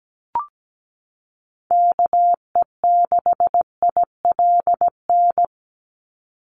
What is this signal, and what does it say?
Morse code, heard as a keyed tone, about 17 wpm, 700 Hz. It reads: KE6ILN
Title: USA Call Signs 17wpm